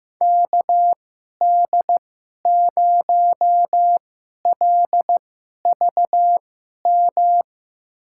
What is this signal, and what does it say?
Morse code, heard as a keyed tone, about 15 wpm, spaced as for 11 wpm, 700 Hz. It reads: KD0LVM
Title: USA Call Signs - 6 Characters - 2x Character Spacing - 15wpm